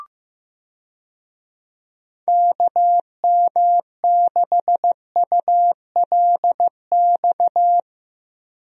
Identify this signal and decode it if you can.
Morse code, heard as a keyed tone, about 16 wpm, 700 Hz. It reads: KM6ULX
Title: USA Call Signs - 6 Characters - Repeated 2x - 15wpm